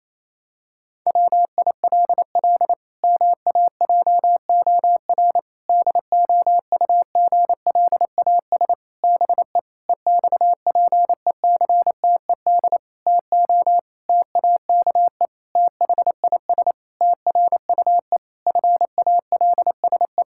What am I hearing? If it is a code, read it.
Morse code, heard as a keyed tone, about 28 wpm, 700 Hz. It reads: WILL MAJOR DOUGLAS BE EXPECTED TO TAKE THIS TRUE FALSE
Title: Pangram Challenge v2 - 28wpm